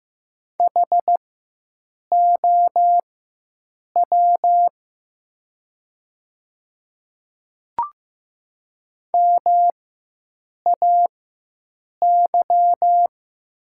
Morse code, heard as a keyed tone, about 15 wpm, spaced as for 7 wpm, 700 Hz. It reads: HOW MAY
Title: 4x Character Spacing - Top 100 Words - 15wpm